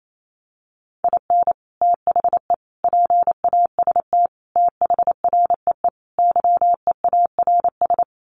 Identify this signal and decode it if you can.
Morse code, heard as a keyed tone, about 28 wpm, 700 Hz. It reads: IN THE PAST THREE YEARS
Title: Sets of 5 Words Encore2 28wpm